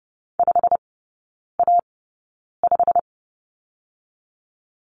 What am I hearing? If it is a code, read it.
Morse code, heard as a keyed tone, about 30 wpm, spaced as for 9 wpm, 700 Hz. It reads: 5A5
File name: course / 30:10wpm / 087 Lesson 15e - Number 5 - Three Characters 30wpm